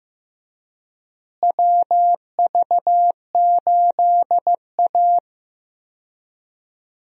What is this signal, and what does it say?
Morse code, heard as a keyed tone, about 15 wpm, 700 Hz. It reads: WV8A
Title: USA Call Signs - 4 Characters - Repeated 3x - 15wpm